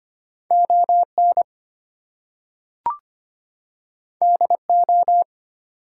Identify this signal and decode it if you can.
Morse code, heard as a keyed tone, about 25 wpm, 700 Hz. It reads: ON DO
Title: Two Letter Words - 25wpm